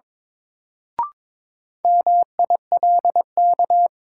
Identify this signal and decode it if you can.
Morse code, heard as a keyed tone, about 22 wpm, 700 Hz. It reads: MILK
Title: Sentences from Top 1500 Words - New Word Drill - 22wpm